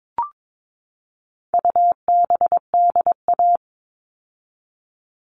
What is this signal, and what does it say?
Morse code, heard as a keyed tone, about 22 wpm, 700 Hz. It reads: UBDA
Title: All Cut Numbers - 4-Digit 22wpm